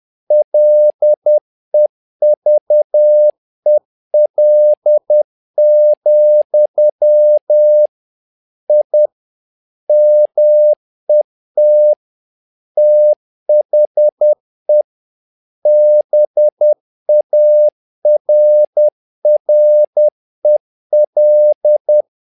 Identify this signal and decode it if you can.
Morse code, heard as a keyed tone, about 10 wpm, 600 Hz. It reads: LEVEL, I MET THE BARREL
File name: learning center / tower trouble / tower trouble 10wpm